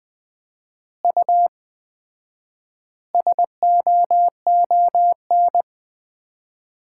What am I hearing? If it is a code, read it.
Morse code, heard as a keyed tone, about 20 wpm, 700 Hz. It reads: U SOON
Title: Common QSO Phrases - 4x Word-Spacing - 20wpm